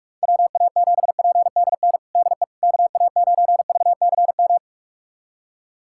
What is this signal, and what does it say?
Morse code, heard as a keyed tone, about 41 wpm, 700 Hz. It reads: WA7PDN DE KA9VCM
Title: USA Callsign DE Callsign 45wpm